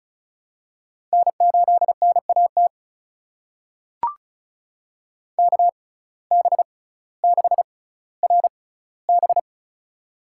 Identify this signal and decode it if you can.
Morse code, heard as a keyed tone, about 36 wpm, spaced as for 18 wpm, 700 Hz. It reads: N8NAT KB6RB
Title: USA Call Signs - 5 Characters - 6x Character Spacing - 35wpm